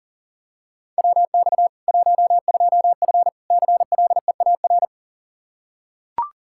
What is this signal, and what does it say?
Morse code, heard as a keyed tone, about 40 wpm, 700 Hz. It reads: WX 12F CLEAR
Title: Practice QSO - Exchange 2 - Info - 40wpm